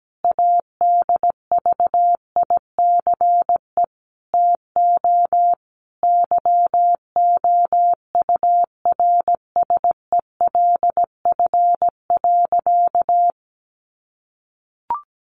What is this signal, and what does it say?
Morse code, heard as a keyed tone, about 17 wpm, 700 Hz. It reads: ADVICE TO YOURSELF.
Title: Sentences from Top 1100 Words 17wpm